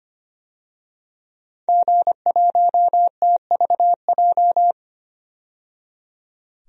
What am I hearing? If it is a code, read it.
Morse code, heard as a keyed tone, about 25 wpm, 700 Hz. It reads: G1TVJ
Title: DXCC-100 Call Signs - 5 Characters - 25wpm